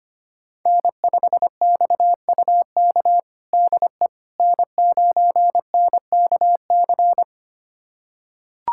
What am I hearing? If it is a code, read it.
Morse code, heard as a keyed tone, about 25 wpm, 700 Hz. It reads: N5XUK DE N9NKC
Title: USA Callsign DE Callsign - 5 Characters - 25wpm